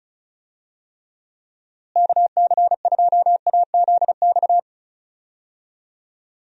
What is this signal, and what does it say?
Morse code, heard as a keyed tone, about 34 wpm, 700 Hz. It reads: KC2AZX
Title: USA Call Signs - 6 Characters - Repeated 3x - 35wpm